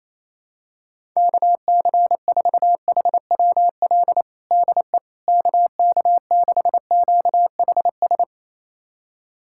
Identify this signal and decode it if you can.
Morse code, heard as a keyed tone, about 28 wpm, 700 Hz. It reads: KC4HWL DE KK6QHS
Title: USA Callsign DE Callsign 28wpm